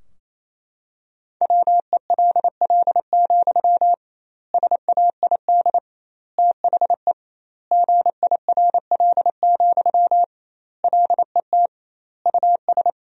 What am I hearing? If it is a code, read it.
Morse code, heard as a keyed tone, about 28 wpm, 700 Hz. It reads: WELL, SAID THE GIRL, LET US
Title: 2x Word-Spacing - Sentences from Top 400 Words - 28wpm